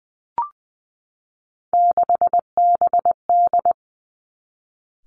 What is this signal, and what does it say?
Morse code, heard as a keyed tone, about 20 wpm, 700 Hz. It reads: 6BD
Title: Three Challenging Letter-Number Combinations 20wpm